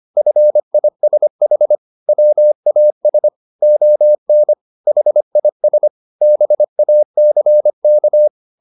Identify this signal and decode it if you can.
Morse code, heard as a keyed tone, about 25 wpm, 600 Hz. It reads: FISH WAS ON HIS BACK